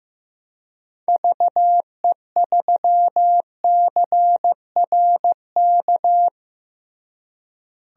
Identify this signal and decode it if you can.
Morse code, heard as a keyed tone, about 15 wpm, 700 Hz. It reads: VE3CRK
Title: DXCC-100 Call Signs - 6 Characters - Repeated 3x - 15wpm